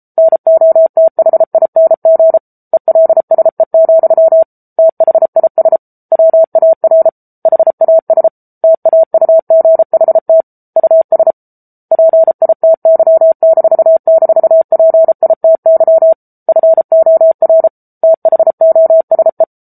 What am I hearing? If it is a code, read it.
Morse code, heard as a keyed tone, about 34 wpm, 650 Hz. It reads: NOTHING ELSE, THIS WAR HAS TAUGHT US PITY--PITY FOR THOSE
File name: Book Two / 07-Chapter 7 - 33 WPM